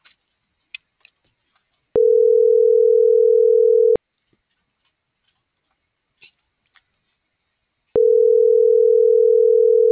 Ringback tone (North America), 440 + 480 Hz.